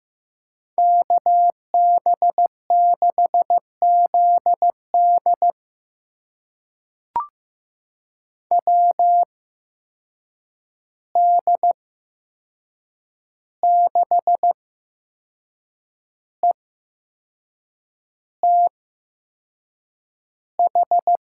Morse code, heard as a keyed tone, about 15 wpm, spaced as for 6 wpm, 700 Hz. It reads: KB6ZD WD6ETH
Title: USA Call Signs - 8x Character Spacing - 15wpm